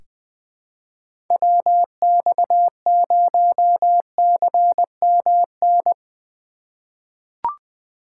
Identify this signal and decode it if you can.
Morse code, heard as a keyed tone, about 20 wpm, 700 Hz. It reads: WX0CMN
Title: USA Call Signs - 6 Characters - 2x Character Spacing - 20wpm